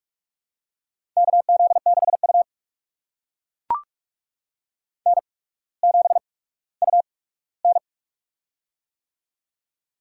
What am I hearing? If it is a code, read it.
Morse code, heard as a keyed tone, about 44 wpm, spaced as for 19 wpm, 700 Hz. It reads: KZ6U N7UN
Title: USA Call Signs - 4 Characters - 8x Character Spacing - 45wpm